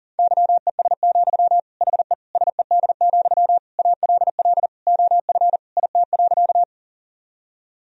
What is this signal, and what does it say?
Morse code, heard as a keyed tone, about 40 wpm, 700 Hz. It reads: YES, HE SED, ALL OF IT.
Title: Sentences from Top 500 Words and Abbreviations 40wpm